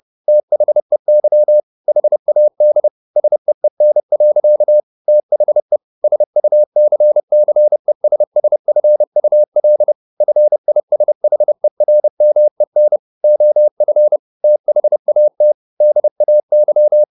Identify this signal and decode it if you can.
Morse code, heard as a keyed tone, about 30 wpm, 600 Hz. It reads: THEY HAD SEEN. THE SUCCESSFUL FISHERMEN OF THAT DAY
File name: the old man and the sea / the old man and the sea 1 30wpm